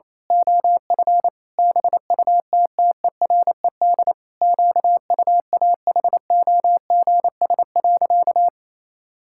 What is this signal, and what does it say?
Morse code, heard as a keyed tone, about 28 wpm, 700 Hz. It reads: OF BUTTERED QUAHOGS.
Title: Pangram Challenge v2 - 28wpm